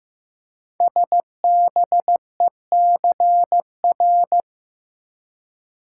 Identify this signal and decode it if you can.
Morse code, heard as a keyed tone, about 15 wpm, 700 Hz. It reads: SBECR
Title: Code Groups - 5 Letters - 15wpm